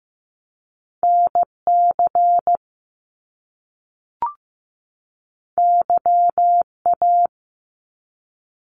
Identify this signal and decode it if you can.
Morse code, heard as a keyed tone, about 15 wpm, 700 Hz. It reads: NC YA
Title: Top Two Letter Combinations in Words 15wpm